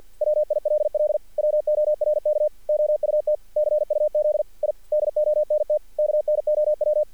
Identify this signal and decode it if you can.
Morse code, heard as a keyed tone, about 48 wpm, 610 Hz. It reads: WILL WORK OUT CUZ I DONT KNOW
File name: archived / CRSnet